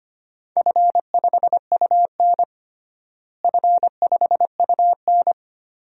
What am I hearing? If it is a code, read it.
Morse code, heard as a keyed tone, about 25 wpm, 700 Hz. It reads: F5UN F5UN
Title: DXCC-100 Call Signs - 4 Characters - Repeated 2x - 25wpm